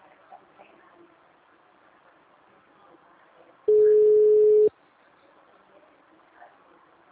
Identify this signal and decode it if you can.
Ringback tone (Europe, 425 Hz).
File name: out- -2001- - - .44854